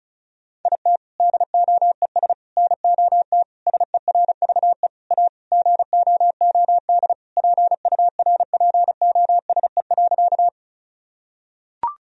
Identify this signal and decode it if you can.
Morse code, heard as a keyed tone, about 36 wpm, 700 Hz. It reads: IT DOES NOT SERVE A GOOD PURPOSE.
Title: Sentences from Top 1000 Words - Review Part 2 - 35wpm